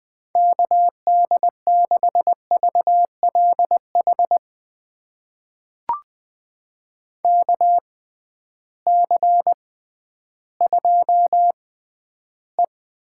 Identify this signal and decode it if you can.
Morse code, heard as a keyed tone, about 21 wpm, spaced as for 12 wpm, 700 Hz. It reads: KD6VLH KC2E
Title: USA Call Signs - 6 Characters - 6x Character Spacing - 20wpm